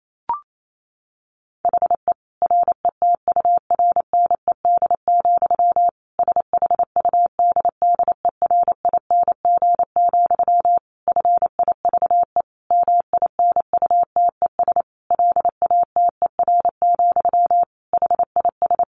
Morse code, heard as a keyed tone, about 28 wpm, 700 Hz. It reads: HE RETURNED, SHUDDERING, FIVE MINUTES LATER, HIS
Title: A Sound Of Thunder 28wpm